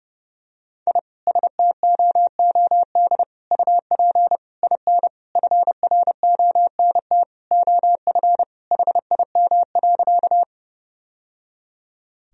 Morse code, heard as a keyed tone, about 30 wpm, 700 Hz. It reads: I STOOD UP IN FRONT OF HIM.
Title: Sentences from Top 1000 Words - Review Part 2 - 30wpm